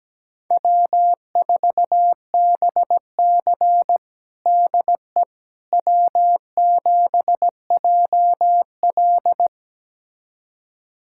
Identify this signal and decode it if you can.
Morse code, heard as a keyed tone, about 17 wpm, 700 Hz. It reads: W4BC DE W7JL
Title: USA Callsign DE Callsign - 4 Characters - 17wpm